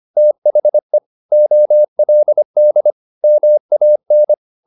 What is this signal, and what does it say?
Morse code, heard as a keyed tone, about 25 wpm, 600 Hz. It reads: THE OLD MAN